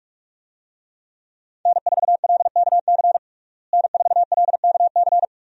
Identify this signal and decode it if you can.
Morse code, heard as a keyed tone, about 46 wpm, 700 Hz. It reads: N4LKC N4LKC
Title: USA Call Signs - 5 Characters - Repeated 2x - 45wpm